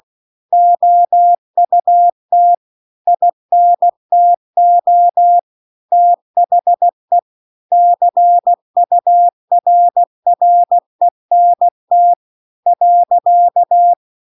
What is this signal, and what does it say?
Morse code, heard as a keyed tone, about 16 wpm, 700 Hz. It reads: OUT INTO THE CURRENT .